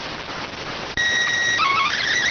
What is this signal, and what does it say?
SSTV (slow-scan television), PD 290.